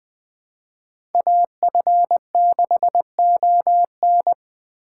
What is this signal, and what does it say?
Morse code, heard as a keyed tone, about 20 wpm, 700 Hz. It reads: AF6ON
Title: USA Call Signs - 5 Characters - 4x Character Spacing - 20wpm